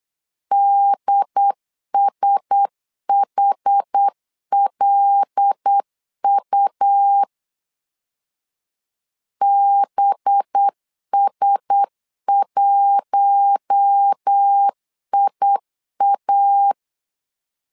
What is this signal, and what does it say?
Morse code, heard as a keyed tone, about 8 wpm, 800 Hz. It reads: DSHLU BS1IA
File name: LEZIONI / 5 LBVC / 5 LB 8wpm 2816